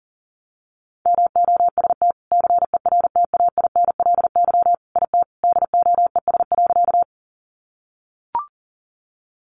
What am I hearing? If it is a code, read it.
Morse code, heard as a keyed tone, about 40 wpm, 700 Hz. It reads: MOST CERTAINLY IT DOES.
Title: Sentences from Top 900 Words 40wpm